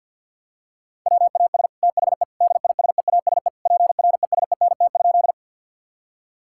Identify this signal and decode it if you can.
Morse code, heard as a keyed tone, about 52 wpm, 700 Hz. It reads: WAS THE DISEASE PRESENT?
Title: Sentences from Top 1000 Words - Review Part 6 - 50wpm